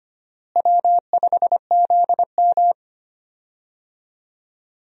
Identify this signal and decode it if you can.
Morse code, heard as a keyed tone, about 25 wpm, 700 Hz. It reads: W5ZM
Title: USA Call Signs - 4 Characters - Repeated 2x - 25wpm